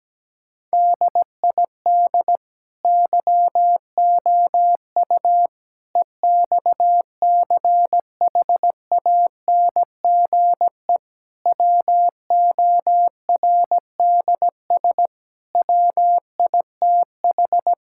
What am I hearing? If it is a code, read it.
Morse code, heard as a keyed tone, about 17 wpm, 700 Hz. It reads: DID YOU EXCHANGE WORDS WITH
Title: Sentences from Top 1000 Words - Review Part 6 - 17wpm